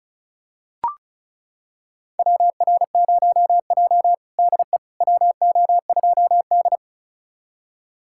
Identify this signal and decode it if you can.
Morse code, heard as a keyed tone, about 36 wpm, 700 Hz. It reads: WR0J DE WO2D
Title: USA Callsign DE Callsign - 4 Characters - 35wpm